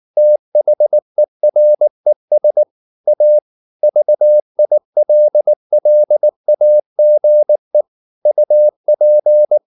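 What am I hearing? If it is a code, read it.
Morse code, heard as a keyed tone, about 19 wpm, 600 Hz. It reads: THERES A VILLAGE UP
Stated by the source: Morse practice at 18 wpm, English text, from CW med Gnister.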